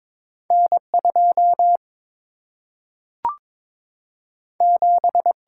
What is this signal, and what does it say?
Morse code, heard as a keyed tone, about 22 wpm, 700 Hz. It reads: N 2 7
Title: Single-Letter Single-Number 22wpm